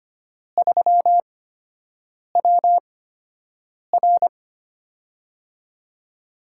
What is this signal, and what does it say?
Morse code, heard as a keyed tone, about 25 wpm, spaced as for 7 wpm, 700 Hz. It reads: 3WR
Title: USA Call Signs - 5 Characters - 8x Character Spacing - 25wpm